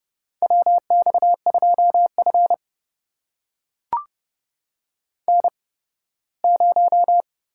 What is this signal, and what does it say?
Morse code, heard as a keyed tone, about 30 wpm, spaced as for 18 wpm, 700 Hz. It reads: WX2F N0
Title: USA Call Signs - 4 Characters - 8x Character Spacing - 30wpm